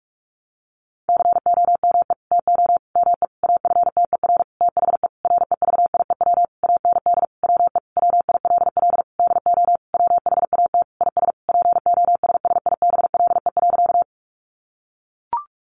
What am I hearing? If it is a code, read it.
Morse code, heard as a keyed tone, about 44 wpm, 700 Hz. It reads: COME TO ME AFTER THE REVIEW AND WE WILL DO WHAT IS POSSIBLE.
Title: Sentences from Top 900 Words 45wpm